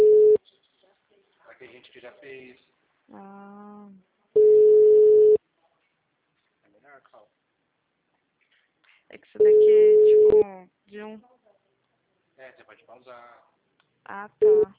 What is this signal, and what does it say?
Ringback tone (Europe, 425 Hz), with speech.
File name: out- -2027- - - .13966